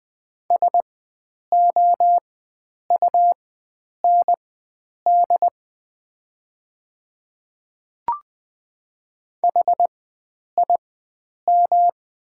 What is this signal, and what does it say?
Morse code, heard as a keyed tone, about 20 wpm, spaced as for 9 wpm, 700 Hz. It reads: SOUND HIM
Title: 4x Character Spacing - Top 100 Words - 20wpm